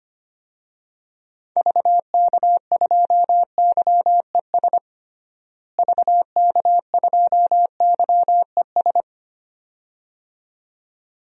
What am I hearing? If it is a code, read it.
Morse code, heard as a keyed tone, about 25 wpm, 700 Hz. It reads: VK2YES VK2YES
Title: DXCC-100 Call Signs - 6 Characters - Repeated 2x - 25wpm